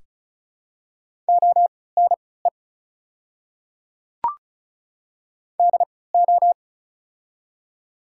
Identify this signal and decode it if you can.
Morse code, heard as a keyed tone, about 35 wpm, spaced as for 20 wpm, 700 Hz. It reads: ONE DO
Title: 3x Character Spacing - Top 200 Words - 35wpm